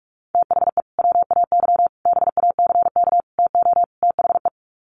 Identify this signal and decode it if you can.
Morse code, heard as a keyed tone, about 44 wpm, 700 Hz. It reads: THE WAY BACK TO THE
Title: Sets of 5 Words Encore2 45wpm